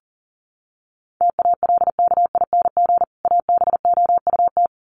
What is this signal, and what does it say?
Morse code, heard as a keyed tone, about 38 wpm, 700 Hz. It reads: TALKING ABOUT
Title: Sets of 2 Words 40wpm